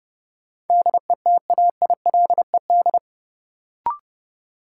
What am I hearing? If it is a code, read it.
Morse code, heard as a keyed tone, about 30 wpm, 700 Hz. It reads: DETAILED
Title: Sentences from Top 1500 Words - New Word Drill - 30wpm